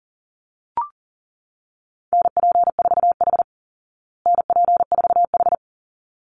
Morse code, heard as a keyed tone, about 40 wpm, 700 Hz. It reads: NP4H NP4H
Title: Call Signs Repeated 3x - 40wpm